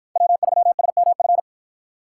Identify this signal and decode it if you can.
Morse code, heard as a keyed tone, about 53 wpm, 700 Hz. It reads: W3SMF